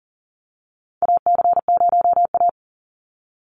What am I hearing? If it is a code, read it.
Morse code, heard as a keyed tone, about 40 wpm, 700 Hz. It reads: AC0A
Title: NA SOTA Chaser Callsigns 40wpm